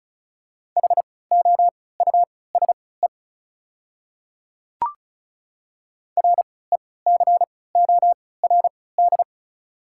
Morse code, heard as a keyed tone, about 36 wpm, spaced as for 20 wpm, 690 Hz. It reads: HOUSE RECORD
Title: 3x Character Spacing - Top 500 Words - 35wpm